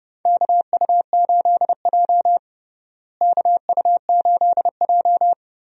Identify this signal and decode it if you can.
Morse code, heard as a keyed tone, about 32 wpm, 700 Hz. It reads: KU8J KU8J
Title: USA Call Signs - 4 Characters - Repeated 2x - 30wpm